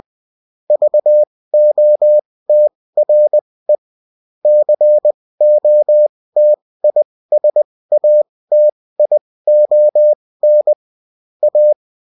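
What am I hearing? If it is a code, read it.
Morse code, heard as a keyed tone, about 20 wpm, spaced as for 16 wpm, 600 Hz. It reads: VOTRE COTISATION A
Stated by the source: Bulletin in text and CW form.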